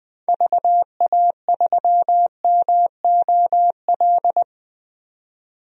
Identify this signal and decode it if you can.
Morse code, heard as a keyed tone, about 21 wpm, 700 Hz. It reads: VA3MOL
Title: DXCC-100 Call Signs - 6 Characters - 20wpm